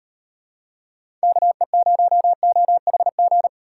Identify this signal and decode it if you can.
Morse code, heard as a keyed tone, about 38 wpm, 700 Hz. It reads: KE0OHG